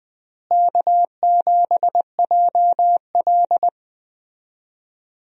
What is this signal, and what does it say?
Morse code, heard as a keyed tone, about 20 wpm, 700 Hz. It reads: K7JL
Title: USA Call Signs - 4 Characters - Repeated 2x - 20wpm